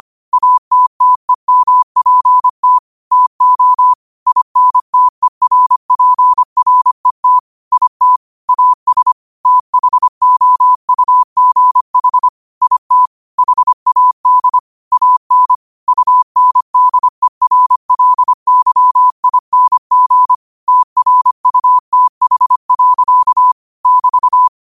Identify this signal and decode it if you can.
Morse code, heard as a keyed tone, about 25 wpm, 1000 Hz. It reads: ATTEMPT TO INTERPRET IT AS THOUGH IT HAD AN UNDERLYING TRUTH. <BT>
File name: 25wpm